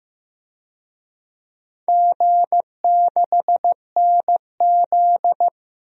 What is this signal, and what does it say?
Morse code, heard as a keyed tone, about 15 wpm, 700 Hz. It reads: G6NZ
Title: DXCC-100 Call Signs - 4 Characters - Repeated 2x - 15wpm